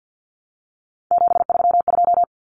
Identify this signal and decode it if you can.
Morse code, heard as a keyed tone, about 50 wpm, 700 Hz. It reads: 732
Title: Numbers - 3-Digit 50wpm